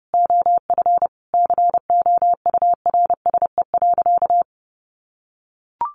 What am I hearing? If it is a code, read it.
Morse code, heard as a keyed tone, about 30 wpm, 700 Hz. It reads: OF COURSE.
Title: Sentences from Top 1100 Words 30wpm